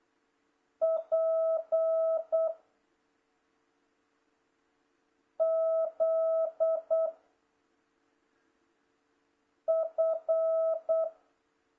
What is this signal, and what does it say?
Morse code, heard as a keyed tone, about 8 wpm, 660 Hz. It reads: P Z F